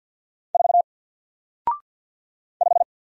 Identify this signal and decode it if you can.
Morse code, heard as a keyed tone, about 47 wpm, 700 Hz. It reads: 4 5
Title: Single Letter-Number Alt 50wpm